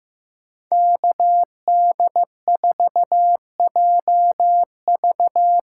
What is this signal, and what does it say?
Morse code, heard as a keyed tone, about 15 wpm, 700 Hz. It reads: KD4JV
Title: USA Call Signs - 6 Characters - 6x Character Spacing - 15wpm